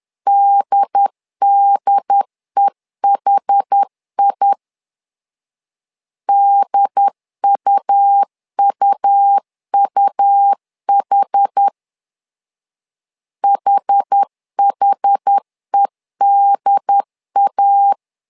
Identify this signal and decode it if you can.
Morse code, heard as a keyed tone, about 10 wpm, 800 Hz. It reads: DDEHI DUUUH HHEDA
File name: LEZIONI / 2 UHDN / 2 UHD 10wpm 1826